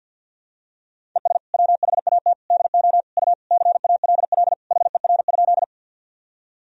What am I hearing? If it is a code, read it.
Morse code, heard as a keyed tone, about 51 wpm, 700 Hz. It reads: ES WHAT DO U CALL HER?
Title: Sentences from Top 100 Words and Abbreviations 50wpm